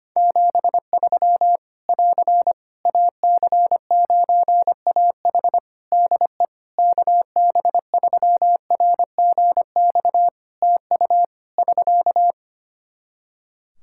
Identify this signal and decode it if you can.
Morse code, heard as a keyed tone, about 25 wpm, 700 Hz. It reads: 73 <AR> AC9AH DE KB3RGX TU <SK>
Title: Practice QSO - Exchange 4 - Ending - 25wpm